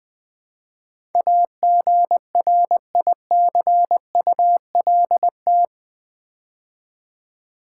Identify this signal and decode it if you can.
Morse code, heard as a keyed tone, about 20 wpm, 700 Hz. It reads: AGRICULT
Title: Build A Word - 4 Letters At A Time - 12-Letter Words - 20wpm